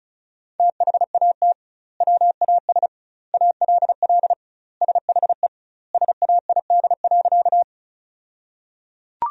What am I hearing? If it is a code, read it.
Morse code, heard as a keyed tone, about 36 wpm, 700 Hz. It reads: THAT WAS ALL SHE SAID.
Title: 2x Word-Spacing - Sentences from Top 100 Words - 35wpm